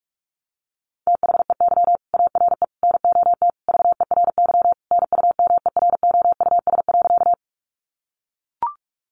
Morse code, heard as a keyed tone, about 44 wpm, 700 Hz. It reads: THEY ARE NOT VERY NUMEROUS.
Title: Sentences from Top 1400 Words 45wpm